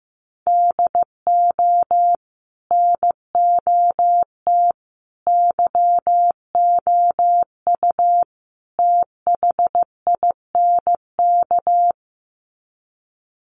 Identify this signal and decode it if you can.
Morse code, heard as a keyed tone, about 15 wpm, 700 Hz. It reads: DO NOT YOU THINK
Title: Sets of 4 Words 15wpm